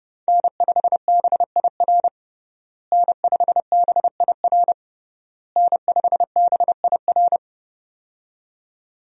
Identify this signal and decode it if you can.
Morse code, heard as a keyed tone, about 30 wpm, 700 Hz. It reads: N5BIR N5BIR N5BIR
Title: USA Call Signs - 5 Characters - Repeated 3x - 30wpm